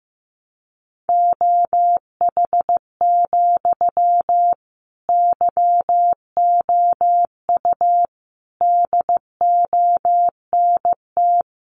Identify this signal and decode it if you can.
Morse code, heard as a keyed tone, about 15 wpm, 700 Hz. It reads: OH, YOU DONT
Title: Sentences from Top 1000 Words 15wpm